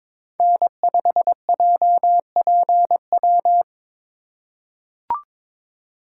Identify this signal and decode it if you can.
Morse code, heard as a keyed tone, about 22 wpm, 700 Hz. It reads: N5JPW
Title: USA Call Signs - 3x Character Spacing - 22wpm